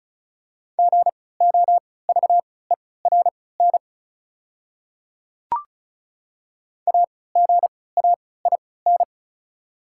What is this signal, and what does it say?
Morse code, heard as a keyed tone, about 36 wpm, spaced as for 20 wpm, 690 Hz. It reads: GOVERN AGAIN
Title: 3x Character Spacing - Top 500 Words - 35wpm